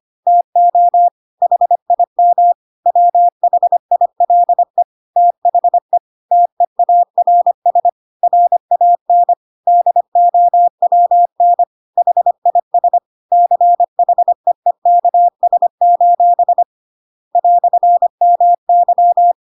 Morse code, heard as a keyed tone, about 25 wpm, 700 Hz. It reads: TO HIM WHILE THE TEARS RAN DOWN HIS CHEEKS: "MY
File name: Pinocchio / Pinocchio ch7 25 25 ik1hge